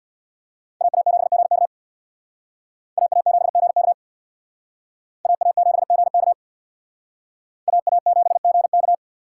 Morse code, heard as a keyed tone, about 61 wpm, 700 Hz. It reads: AA7GK AA7GK AA7GK AA7GK
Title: Speed Racing - USA 2x2 Call Signs 50wpm